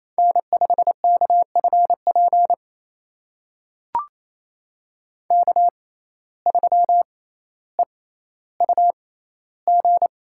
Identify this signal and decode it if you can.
Morse code, heard as a keyed tone, about 28 wpm, spaced as for 14 wpm, 700 Hz. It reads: N5KFP K3EUG
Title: USA Call Signs - 5 Characters - 6x Character Spacing - 28wpm